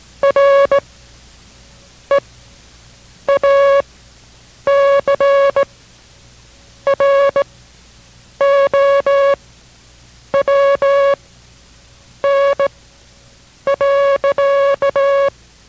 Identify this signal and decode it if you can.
Morse code, heard as a keyed tone, about 17 wpm, spaced as for 7 wpm, 560 Hz. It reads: RE A CROWN.